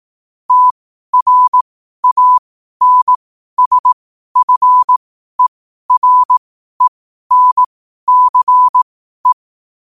Morse code, heard as a keyed tone, about 18 wpm, spaced as for 12 wpm, 1000 Hz. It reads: TRANSFERENCE
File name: qotdfiles / 12wpm